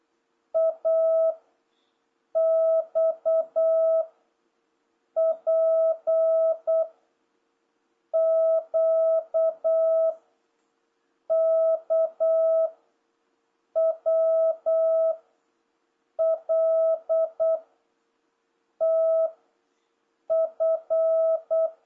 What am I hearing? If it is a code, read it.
Morse code, heard as a keyed tone, about 8 wpm, 650 Hz. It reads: A X P Q K W L T F